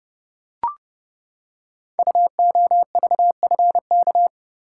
Morse code, heard as a keyed tone, about 28 wpm, 700 Hz. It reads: UOVFK
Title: Code Groups - 5 Letters - 30wpm